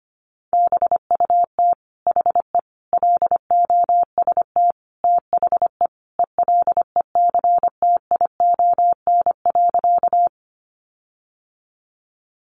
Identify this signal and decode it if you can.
Morse code, heard as a keyed tone, about 25 wpm, 700 Hz. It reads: BUT HE LOST THE ELECTION.
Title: Sentences from Top 1500 Words 25wpm